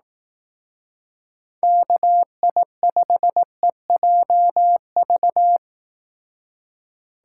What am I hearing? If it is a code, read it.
Morse code, heard as a keyed tone, about 18 wpm, 700 Hz. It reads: KI5EJV
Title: Speed Racing - USA Call Signs 15wpm